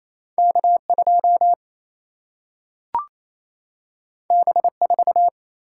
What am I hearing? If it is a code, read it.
Morse code, heard as a keyed tone, about 28 wpm, 700 Hz. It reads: K2 B4
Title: Single-Letter Single-Number 28wpm